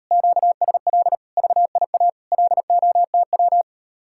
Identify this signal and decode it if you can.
Morse code, heard as a keyed tone, about 38 wpm, 700 Hz. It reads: QSL VIA LOTW